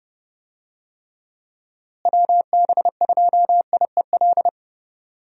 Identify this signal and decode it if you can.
Morse code, heard as a keyed tone, about 30 wpm, 700 Hz. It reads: WB2IEL
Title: USA Call Signs - 6 Characters - Repeated 2x - 30wpm